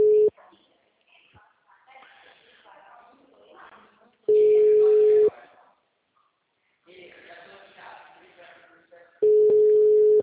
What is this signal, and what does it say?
Ringback tone (Europe, 425 Hz), with speech.